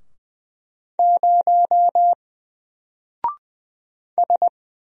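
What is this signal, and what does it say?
Morse code, heard as a keyed tone, about 20 wpm, 700 Hz. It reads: 0 S
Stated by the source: Single letter and number Morse drill at 20 wpm, from N4LCD.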